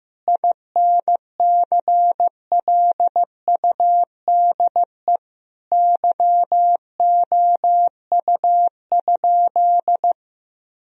Morse code, heard as a keyed tone, about 15 wpm, 700 Hz. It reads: INCLUDE YOU?
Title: Sentences from Top 1000 Words - Review Part 3 - 15wpm